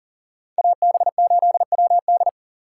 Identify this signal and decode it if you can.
Morse code, heard as a keyed tone, about 40 wpm, 700 Hz. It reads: AB8WD
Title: USA Call Signs - 4x Character Spacing - 40wpm